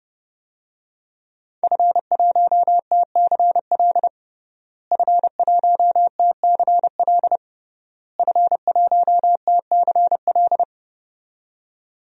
Morse code, heard as a keyed tone, about 32 wpm, 700 Hz. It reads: F1TCL F1TCL F1TCL
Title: DXCC-100 Call Signs - 5 Characters - Repeated 3x - 30wpm